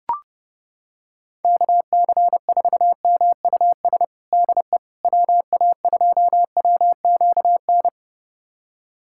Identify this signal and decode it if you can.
Morse code, heard as a keyed tone, about 30 wpm, 700 Hz. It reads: KC4MUS DE WA2WQN
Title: USA Callsign DE Callsign 30wpm